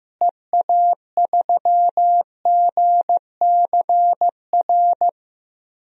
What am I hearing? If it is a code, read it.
Morse code, heard as a keyed tone, about 14 wpm, 700 Hz. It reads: EA3GCR
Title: DXCC-100 Call Signs - 6 Characters - Repeated 3x - 15wpm